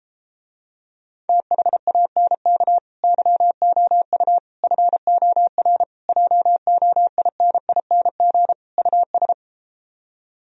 Morse code, heard as a keyed tone, about 33 wpm, 700 Hz. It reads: THANK YOU FOR JOINING US